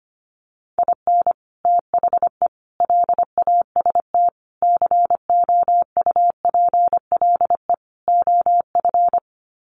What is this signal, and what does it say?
Morse code, heard as a keyed tone, about 25 wpm, 700 Hz. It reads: IN THE LAST COUPLE OF
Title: Sets of 5 Words 25wpm